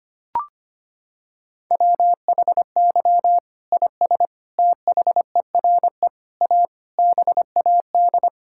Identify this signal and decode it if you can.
Morse code, heard as a keyed tone, about 25 wpm, 700 Hz. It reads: WHY IS THERE A BAD
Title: Sentences from Top 1600 Words 25wpm